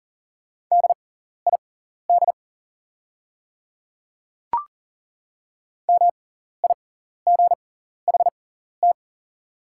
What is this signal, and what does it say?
Morse code, heard as a keyed tone, about 40 wpm, spaced as for 14 wpm, 690 Hz. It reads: DID MIGHT
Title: 6x Character Spacing - Top 500 Words - 40wpm